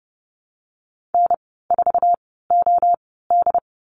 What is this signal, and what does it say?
Morse code, heard as a keyed tone, about 30 wpm, spaced as for 17 wpm, 700 Hz. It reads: N4OD
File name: Lesson 15g - Number 5 - Call Signs 30wpm